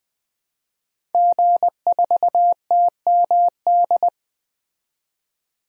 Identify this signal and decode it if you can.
Morse code, heard as a keyed tone, about 20 wpm, 700 Hz. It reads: G4TMD
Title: DXCC-100 Call Signs - 5 Characters - 20wpm